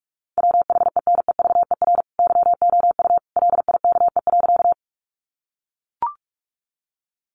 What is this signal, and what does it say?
Morse code, heard as a keyed tone, about 44 wpm, 700 Hz. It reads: WHENEVER YOU LIKE.
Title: Sentences from Top 1200 Words 45wpm